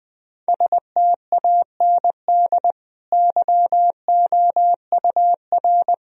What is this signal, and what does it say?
Morse code, heard as a keyed tone, about 20 wpm, 700 Hz. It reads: STAND YOUR
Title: Sentences from Top 1000 Words - Review Part 2 - 20wpm